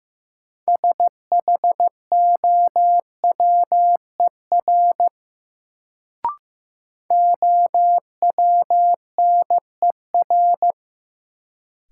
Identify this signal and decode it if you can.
Morse code, heard as a keyed tone, about 15 wpm, 700 Hz. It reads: SHOWER OWNER
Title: Sentences from Top 1600 Words - New Word Drill Alt - 15wpm